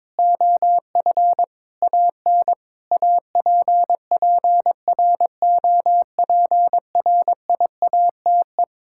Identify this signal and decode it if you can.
Morse code, heard as a keyed tone, about 22 wpm, 700 Hz. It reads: OF AN APPROPRIATE
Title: Sentences from Top 1000 Words - Review Part 5 - 22wpm